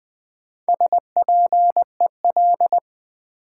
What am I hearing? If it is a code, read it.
Morse code, heard as a keyed tone, about 20 wpm, 700 Hz. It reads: SPEL
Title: Build A Word - 1 Letter At A Time - 5-Letter Words - 20wpm